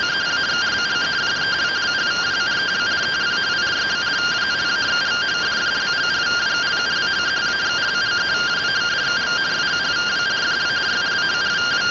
Two-tone teleprinter signal, about 170 Hz shift.